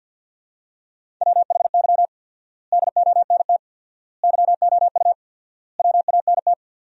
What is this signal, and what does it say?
Morse code, heard as a keyed tone, about 46 wpm, 700 Hz. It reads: WHY DONT YOU WANT
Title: 4x Word-Spacing - Sentences from Top 300 Words - 50wpm